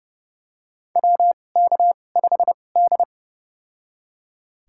Morse code, heard as a keyed tone, about 30 wpm, spaced as for 22 wpm, 700 Hz. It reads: WK5D
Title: USA Call Signs - 2x Character Spacing - 30wpm